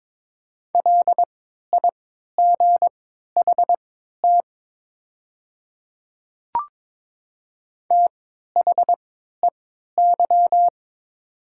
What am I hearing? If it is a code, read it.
Morse code, heard as a keyed tone, about 22 wpm, spaced as for 12 wpm, 700 Hz. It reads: LIGHT THEY
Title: 3x Character Spacing - Top 200 Words - 22wpm